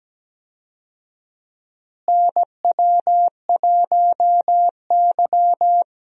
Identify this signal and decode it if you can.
Morse code, heard as a keyed tone, about 17 wpm, 700 Hz. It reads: NW1Y
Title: USA Call Signs - 4 Characters - Repeated 2x - 17wpm